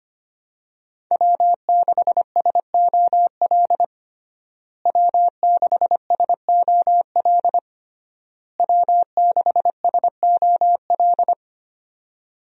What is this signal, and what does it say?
Morse code, heard as a keyed tone, about 25 wpm, 700 Hz. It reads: W6SOL W6SOL W6SOL
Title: USA Call Signs - 5 Characters - Repeated 3x - 25wpm